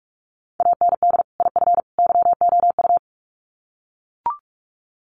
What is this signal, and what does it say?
Morse code, heard as a keyed tone, about 44 wpm, 700 Hz. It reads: AND IF YOU
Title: Sets of 3 Words 45wpm